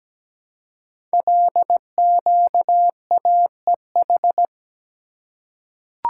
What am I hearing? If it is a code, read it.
Morse code, heard as a keyed tone, about 17 wpm, 700 Hz. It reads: LQAEH
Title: Code Groups - 5 Letters - 17wpm